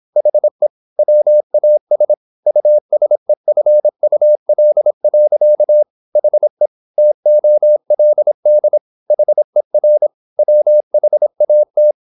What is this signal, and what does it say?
Morse code, heard as a keyed tone, about 26 wpm, 600 Hz. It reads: HE WAS USEFUL. HE TOLD HER WHAT
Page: Never 26wpm | CW med Gnister